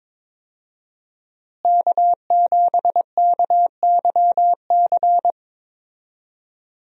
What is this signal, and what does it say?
Morse code, heard as a keyed tone, about 22 wpm, 700 Hz. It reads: K7KYC
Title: USA Call Signs - 5 Characters - Repeated 2x - 22wpm